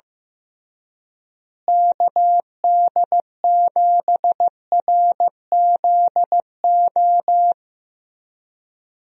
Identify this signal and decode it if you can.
Morse code, heard as a keyed tone, about 16 wpm, 700 Hz. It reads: KD7RZO
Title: USA Call Signs - 6 Characters - Repeated 3x - 15wpm